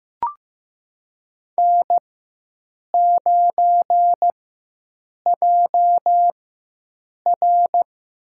Morse code, heard as a keyed tone, about 15 wpm, spaced as for 7 wpm, 700 Hz. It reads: N9JR
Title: USA Call Signs - 4x Character Spacing - 15wpm